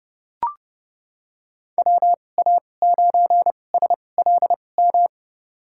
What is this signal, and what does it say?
Morse code, heard as a keyed tone, about 32 wpm, spaced as for 22 wpm, 700 Hz. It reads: WA9SLM
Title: USA Call Signs - 2x Character Spacing - 30wpm